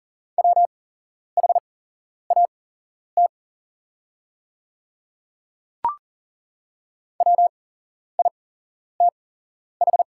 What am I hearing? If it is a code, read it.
Morse code, heard as a keyed tone, about 40 wpm, spaced as for 11 wpm, 700 Hz. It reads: WHAT WITH
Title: 8x Character Spacing - Top 100 Words - 40wpm